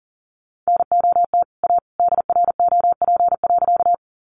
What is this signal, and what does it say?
Morse code, heard as a keyed tone, about 40 wpm, 700 Hz. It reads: NOT A DROP.
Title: Sentences from Top 1400 Words 40wpm